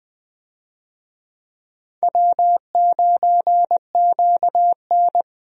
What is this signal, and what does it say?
Morse code, heard as a keyed tone, about 21 wpm, 700 Hz. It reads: W9QN
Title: USA Call Signs - 4 Characters - Repeated 2x - 20wpm